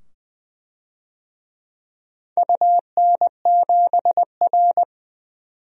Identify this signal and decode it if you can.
Morse code, heard as a keyed tone, about 20 wpm, 700 Hz. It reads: UN7R
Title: DXCC-100 Call Signs - 4 Characters - 20wpm